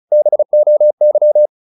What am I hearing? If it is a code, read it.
Morse code, heard as a keyed tone, about 34 wpm, 600 Hz. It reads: BOY
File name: the old man and the sea 2 35wpm